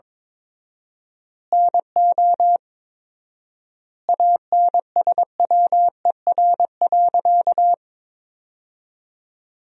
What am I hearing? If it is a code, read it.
Morse code, heard as a keyed tone, about 21 wpm, 700 Hz. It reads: NO ANSWER.
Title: 4x Word-Spacing - Sentences from Top 500 Words - 22wpm